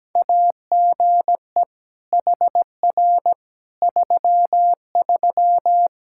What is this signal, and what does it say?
Morse code, heard as a keyed tone, about 17 wpm, 700 Hz. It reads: AGE HR 33
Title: K1USN SST QSO Practice 17wpm